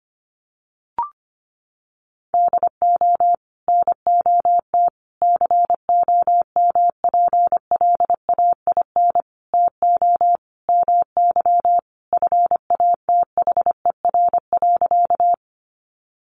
Morse code, heard as a keyed tone, about 25 wpm, 700 Hz. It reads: DO NOT COMPLAIN TO MY FATHER.
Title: Sentences from Top 1300 Words 25wpm